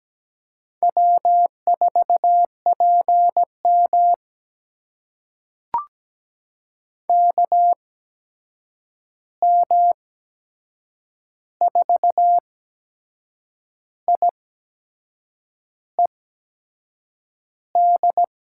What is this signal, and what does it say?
Morse code, heard as a keyed tone, about 17 wpm, spaced as for 6 wpm, 700 Hz. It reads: W4PM KM4IED
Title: USA Call Signs - 8x Character Spacing - 17wpm